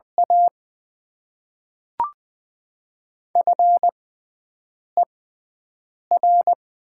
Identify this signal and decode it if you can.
Morse code, heard as a keyed tone, about 20 wpm, spaced as for 7 wpm, 700 Hz. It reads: A FER
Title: Top 20 QSO Elements - 6x Character Spacing - 20wpm